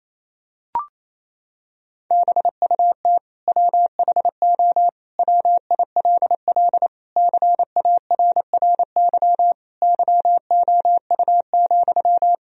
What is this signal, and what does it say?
Morse code, heard as a keyed tone, about 28 wpm, 700 Hz. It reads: BUT WHO WILL CARRY YOU,
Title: Sentences from Top 1000 Words - Review Part 1 - 28wpm